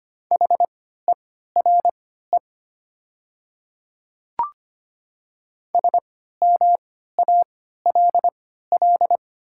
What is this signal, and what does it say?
Morse code, heard as a keyed tone, about 25 wpm, spaced as for 14 wpm, 700 Hz. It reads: HERE SMALL
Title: 3x Character Spacing - Top 200 Words - 25wpm